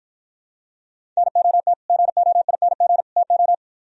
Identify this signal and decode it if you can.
Morse code, heard as a keyed tone, about 60 wpm, 700 Hz. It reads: NOT GOING TO